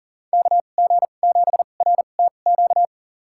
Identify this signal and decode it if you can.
Morse code, heard as a keyed tone, about 40 wpm, spaced as for 29 wpm, 700 Hz. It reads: KG7RTQ
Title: USA Call Signs - 6 Characters - 2x Character Spacing - 40wpm